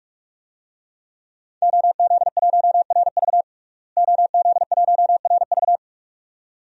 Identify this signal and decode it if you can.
Morse code, heard as a keyed tone, about 42 wpm, 700 Hz. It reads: OZ1RV OZ1RV
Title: DXCC-100 Call Signs - 5 Characters - Repeated 2x - 45wpm